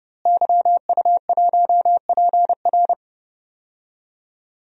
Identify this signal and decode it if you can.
Morse code, heard as a keyed tone, about 32 wpm, 700 Hz. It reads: YU1PR
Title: DXCC-100 Call Signs - 5 Characters - Repeated 2x - 30wpm